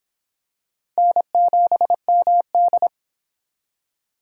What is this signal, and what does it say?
Morse code, heard as a keyed tone, about 23 wpm, 700 Hz. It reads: N7MD